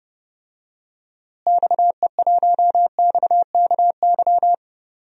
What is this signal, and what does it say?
Morse code, heard as a keyed tone, about 32 wpm, 700 Hz. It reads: XE1XKY
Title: DXCC-100 Call Signs - 6 Characters - Repeated 3x - 30wpm